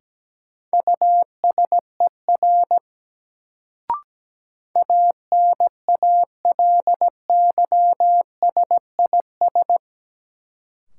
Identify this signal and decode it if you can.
Morse code, heard as a keyed tone, about 17 wpm, 700 Hz. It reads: USER ANALYSIS
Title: Sentences from Top 1000 Words - Review Part 5 - New Word Drill - 17wpm